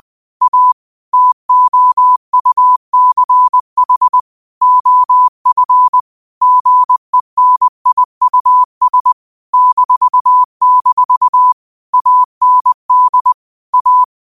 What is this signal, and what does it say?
Morse code, heard as a keyed tone, about 20 wpm, 1000 Hz. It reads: A TOUCH OF GENIUS -- AND A
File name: qotdfiles / 20wpm